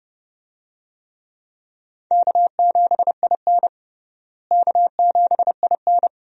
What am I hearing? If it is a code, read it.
Morse code, heard as a keyed tone, about 32 wpm, 700 Hz. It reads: K7IN K7IN
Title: USA Call Signs - 4 Characters - Repeated 2x - 30wpm